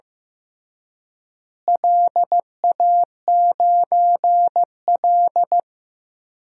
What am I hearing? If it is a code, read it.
Morse code, heard as a keyed tone, about 15 wpm, 700 Hz. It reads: LA9L
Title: DXCC-100 Call Signs - 4 Characters - Repeated 3x - 15wpm